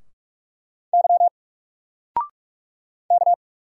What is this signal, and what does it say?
Morse code, heard as a keyed tone, about 44 wpm, 700 Hz. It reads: Y K
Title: Single Character (Letter:Number:Punctuation:Prosign) 45wpm